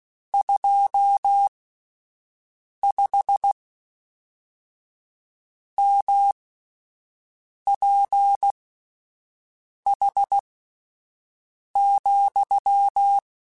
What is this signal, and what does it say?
Morse code, heard as a keyed tone, about 16 wpm, spaced as for 5 wpm, 800 Hz. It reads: 25 MPH,
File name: Spaced Characters / Spaced Characters-mp3s / Weather5 5 sp @10